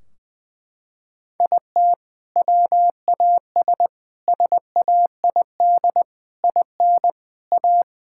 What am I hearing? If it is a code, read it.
Morse code, heard as a keyed tone, about 20 wpm, 700 Hz. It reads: IT WAS SAID IN A
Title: Sentences from Top 1600 Words 20wpm